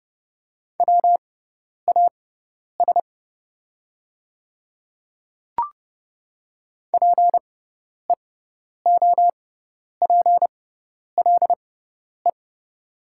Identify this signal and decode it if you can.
Morse code, heard as a keyed tone, about 30 wpm, spaced as for 10 wpm, 700 Hz. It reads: WAS PEOPLE
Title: 6x Character Spacing - Top 100 Words - 30wpm